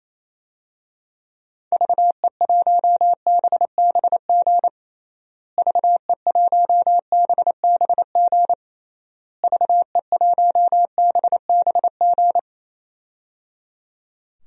Morse code, heard as a keyed tone, about 28 wpm, 700 Hz. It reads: VE1BBG VE1BBG VE1BBG
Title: DXCC-100 Call Signs - 6 Characters - Repeated 3x - 28wpm